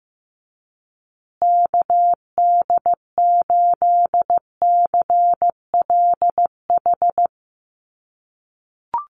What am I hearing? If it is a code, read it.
Morse code, heard as a keyed tone, about 15 wpm, 700 Hz. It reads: KD8CLH
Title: USA Call Signs Repeated 2x 15wpm